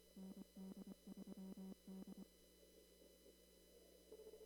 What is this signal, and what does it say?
Morse code, heard as a keyed tone, about 23 wpm, 190 Hz. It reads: ND3D
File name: CW / kt5j wpx cw-op-k5pi